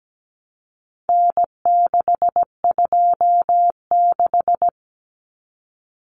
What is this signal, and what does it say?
Morse code, heard as a keyed tone, about 17 wpm, 700 Hz. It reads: N626
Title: Common Cut Numbers - 4-Digit 17wpm